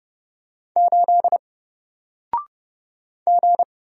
Morse code, heard as a keyed tone, about 30 wpm, 700 Hz. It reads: 8 G
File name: Single-Letter-Number-30wpm